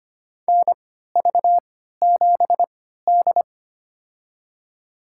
Morse code, heard as a keyed tone, about 25 wpm, spaced as for 14 wpm, 700 Hz. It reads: NV7D
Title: USA Call Signs - 3x Character Spacing - 25wpm